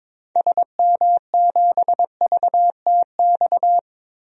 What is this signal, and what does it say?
Morse code, heard as a keyed tone, about 22 wpm, 700 Hz. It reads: SM7VTX
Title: DXCC-100 Call Signs - 6 Characters - 22wpm